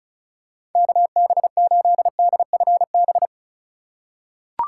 Morse code, heard as a keyed tone, about 36 wpm, 700 Hz. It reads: KB8DFB
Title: USA Call Signs - 6 Characters - 3x Character Spacing - 35wpm